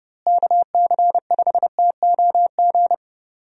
Morse code, heard as a keyed tone, about 30 wpm, 700 Hz. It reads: KC5TOG
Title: USA Call Signs - 6 Characters - Repeated 2x - 30wpm